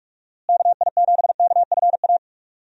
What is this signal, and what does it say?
Morse code, heard as a keyed tone, about 42 wpm, 700 Hz. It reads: KI7KFA
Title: USA Call Signs - 6 Characters - Repeated 3x - 45wpm